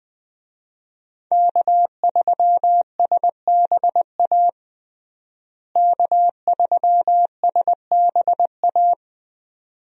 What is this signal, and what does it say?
Morse code, heard as a keyed tone, about 20 wpm, 700 Hz. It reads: K3SBA K3SBA
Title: USA Call Signs - 5 Characters - Repeated 2x - 20wpm